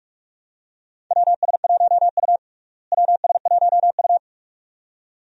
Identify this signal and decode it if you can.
Morse code, heard as a keyed tone, about 46 wpm, 700 Hz. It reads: WS1U WS1U
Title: USA Call Signs - 4 Characters - Repeated 2x - 45wpm